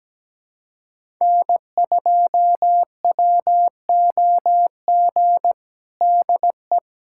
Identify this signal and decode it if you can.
Morse code, heard as a keyed tone, about 17 wpm, 700 Hz. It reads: N2WOG DE
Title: USA Callsign DE Callsign - 5 Characters - 17wpm